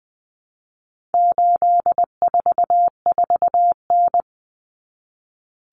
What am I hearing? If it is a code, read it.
Morse code, heard as a keyed tone, about 20 wpm, 700 Hz. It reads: 844N
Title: Common Cut Numbers - 4-Digit 20wpm